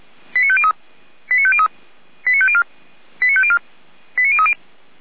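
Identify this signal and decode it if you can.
Five-tone selective call (ZVEI), heard as quick runs of five tones, tones of about 80 ms. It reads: 89503 89503 89504 89504 89030
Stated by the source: ZVEI five-tone call received on SDR, from the Feuerwehr Orlamünde server.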